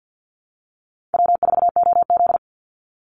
Morse code, heard as a keyed tone, about 50 wpm, 700 Hz. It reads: W4OZ
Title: USA Call Signs 50wpm